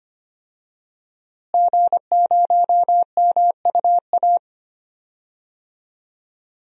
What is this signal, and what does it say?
Morse code, heard as a keyed tone, about 25 wpm, 700 Hz. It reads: G0MUA
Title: DXCC-100 Call Signs - 5 Characters - Repeated 3x - 25wpm